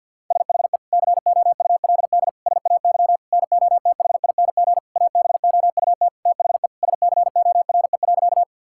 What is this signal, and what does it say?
Morse code, heard as a keyed tone, about 50 wpm, 700 Hz. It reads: SHE COULD SAY NOTHING ABOUT THE SCORE.
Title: Sentences from Top 1600 Words 50wpm